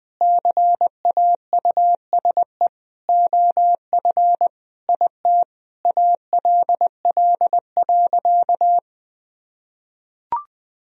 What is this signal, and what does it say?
Morse code, heard as a keyed tone, about 20 wpm, 700 Hz. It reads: CAUSE OF IT ALL.
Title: Sentences from Top 1000 Words - Review Part 1 - 20wpm